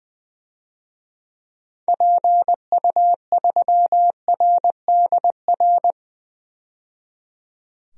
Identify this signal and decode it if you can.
Morse code, heard as a keyed tone, about 20 wpm, 700 Hz. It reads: PU3RDR
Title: DXCC-100 Call Signs - 6 Characters - 20wpm